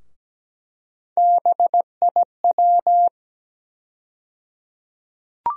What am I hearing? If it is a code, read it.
Morse code, heard as a keyed tone, about 17 wpm, 700 Hz. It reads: BIW
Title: USA Call Sign Postfixes - 3 Letters 17wpm